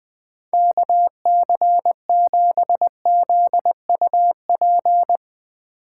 Morse code, heard as a keyed tone, about 20 wpm, 700 Hz. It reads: KC7ZUP
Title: Speed Racing - USA 2x3 Call Signs 17wpm